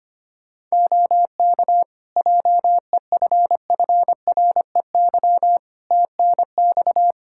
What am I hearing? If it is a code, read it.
Morse code, heard as a keyed tone, about 25 wpm, 700 Hz. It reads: OK JEFFREY TNX
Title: Practice QSO - Exchange 4 - Ending - 25wpm